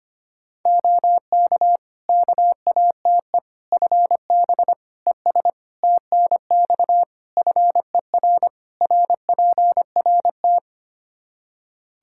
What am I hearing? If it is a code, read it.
Morse code, heard as a keyed tone, about 25 wpm, 700 Hz. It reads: OK KATE FB ES TNX FER RPRT
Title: Practice QSO - Exchange 2 - Info - 25wpm